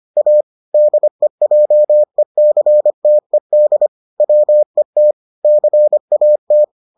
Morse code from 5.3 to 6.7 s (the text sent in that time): CAT